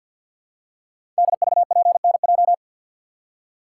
Morse code, heard as a keyed tone, about 52 wpm, 700 Hz. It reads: DVPNJ
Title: Code Groups - 5 Letters - 50wpm